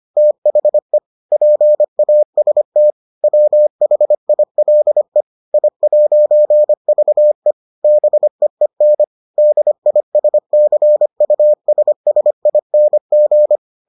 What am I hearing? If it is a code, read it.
Morse code, heard as a keyed tone, about 25 wpm, 600 Hz. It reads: THE PAST WHILE I'VE BEEN DISCUSSING